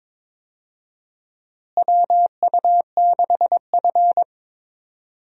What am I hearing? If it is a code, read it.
Morse code, heard as a keyed tone, about 22 wpm, 700 Hz. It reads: WU6F
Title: USA Call Signs - 4 Characters - Repeated 3x - 22wpm